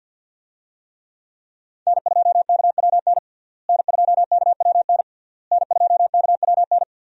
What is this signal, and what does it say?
Morse code, heard as a keyed tone, about 51 wpm, 700 Hz. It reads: N2KWN N2KWN N2KWN
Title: USA Call Signs - 5 Characters - Repeated 3x - 50wpm